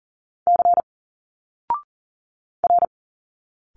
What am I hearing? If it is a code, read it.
Morse code, heard as a keyed tone, about 40 wpm, 700 Hz. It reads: C R
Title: Single Letters 40wpm